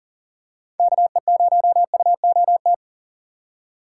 Morse code, heard as a keyed tone, about 40 wpm, 700 Hz. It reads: KE0UOT
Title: USA Call Signs - 6 Characters - 3x Character Spacing - 40wpm